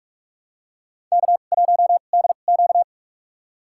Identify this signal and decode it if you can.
Morse code, heard as a keyed tone, about 46 wpm, spaced as for 33 wpm, 700 Hz. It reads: K1DQ
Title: USA Call Signs - 2x Character Spacing - 45wpm